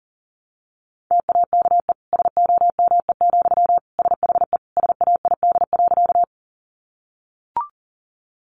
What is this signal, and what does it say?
Morse code, heard as a keyed tone, about 40 wpm, 700 Hz. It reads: TAKE SOME, SHE SAID.
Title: Sentences from Top 200 Words 40wpm